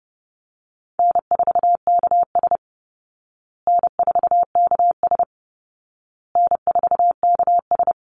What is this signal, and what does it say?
Morse code, heard as a keyed tone, about 30 wpm, 700 Hz. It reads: N4KS N4KS N4KS
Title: Easy Call Signs Repeated 3x - 30wpm